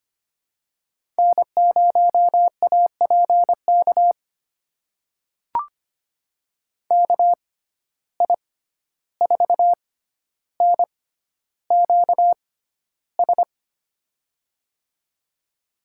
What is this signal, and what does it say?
Morse code, heard as a keyed tone, about 25 wpm, spaced as for 12 wpm, 700 Hz. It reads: N0APK KI4NQS
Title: USA Call Signs - 6x Character Spacing - 25wpm